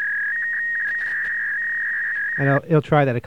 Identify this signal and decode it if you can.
Fax machine handshake, with speech.